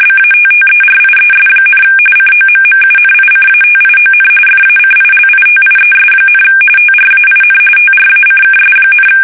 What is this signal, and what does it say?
RTTY, about 45.45 baud. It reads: BATTERIES, COMMUNICATION FACILITIES, TRAIN TRACKS,
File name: fsk850 / battleshipnj